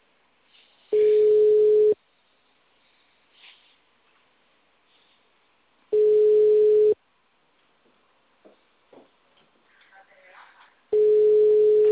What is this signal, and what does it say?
Ringback tone (Europe, 425 Hz).